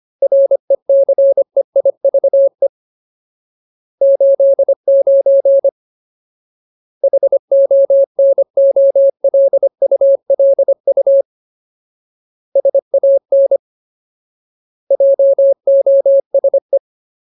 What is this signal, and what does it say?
Morse code, heard as a keyed tone, about 25 wpm, 550 Hz. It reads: RECEIVE 89 HONOLULU SAN JOSE
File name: Ultimate Word List / 25WPM Ultimate